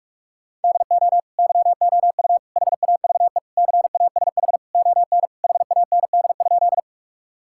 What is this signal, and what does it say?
Morse code, heard as a keyed tone, about 46 wpm, 700 Hz. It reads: DO YOU HAVE CASH ON HAND?
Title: Sentences from Top 1000 Words - Review Part 5 - 45wpm